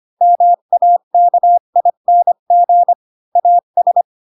Morse code, heard as a keyed tone, about 25 wpm, 700 Hz. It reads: MAKING AS
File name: Pinocchio / Pinocchio ch3 20 25 ik1hge